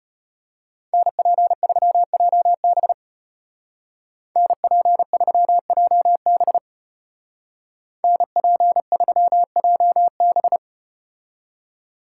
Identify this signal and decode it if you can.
Morse code, heard as a keyed tone, about 35 wpm, 700 Hz. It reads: NP3JB NP3JB NP3JB